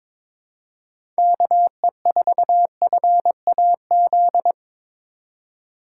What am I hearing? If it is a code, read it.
Morse code, heard as a keyed tone, about 21 wpm, 700 Hz. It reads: KE4FAZ
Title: USA Call Signs - 6 Characters - 2x Character Spacing - 22wpm